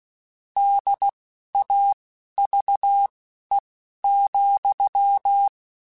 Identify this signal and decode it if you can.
Morse code, heard as a keyed tone, about 16 wpm, spaced as for 11 wpm, 800 Hz. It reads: DAVE,
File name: (spaced) / QSOs Spaced-mp3s / kc7sac @15 sp